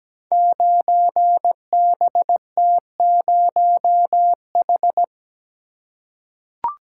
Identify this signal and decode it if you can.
Morse code, heard as a keyed tone, about 17 wpm, 700 Hz. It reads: 9BT0H
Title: Code Groups - 5 Letter:Numbers - 17wpm